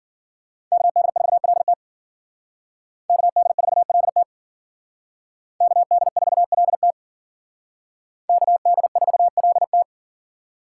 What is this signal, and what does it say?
Morse code, heard as a keyed tone, about 51 wpm, 700 Hz. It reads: KD4LT KD4LT KD4LT KD4LT